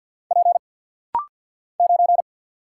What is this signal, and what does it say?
Morse code, heard as a keyed tone, about 50 wpm, 690 Hz. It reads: P 9
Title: Single Letter-Number Alt - Rapid-Fire 50wpm